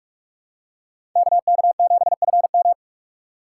Morse code, heard as a keyed tone, about 46 wpm, 700 Hz. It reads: KK7FM
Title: USA Call Signs - 5 Characters - 3x Character Spacing - 45wpm